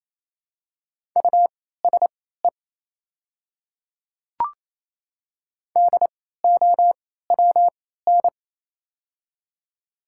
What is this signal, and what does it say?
Morse code, heard as a keyed tone, about 28 wpm, spaced as for 16 wpm, 700 Hz. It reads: USE DOWN
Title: 3x Character Spacing - Top 300 Words - 28wpm